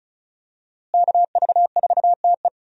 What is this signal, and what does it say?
Morse code, heard as a keyed tone, about 36 wpm, 700 Hz. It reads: KV4TE
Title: USA Call Signs - 5 Characters - Repeated 2x - 35wpm